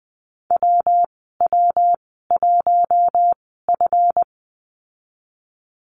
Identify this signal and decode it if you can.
Morse code, heard as a keyed tone, about 20 wpm, spaced as for 14 wpm, 700 Hz. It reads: WW1F
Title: 154 Lesson 25g - Letter p - Call Signs 20wpm